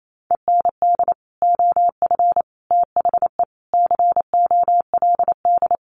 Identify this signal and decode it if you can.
Morse code, heard as a keyed tone, about 28 wpm, 700 Hz. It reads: END OF THE COLD
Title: Sets of 5 Words Encore 28wpm